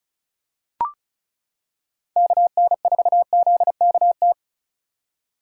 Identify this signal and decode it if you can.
Morse code, heard as a keyed tone, about 34 wpm, 700 Hz. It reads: KN4ZKT
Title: K1USN SST QSO Practice - 35wpm